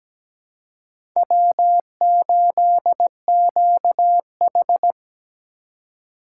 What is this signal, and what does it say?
Morse code, heard as a keyed tone, about 17 wpm, 700 Hz. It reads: W8QH
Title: USA Call Signs - 4 Characters - 2x Character Spacing - 17wpm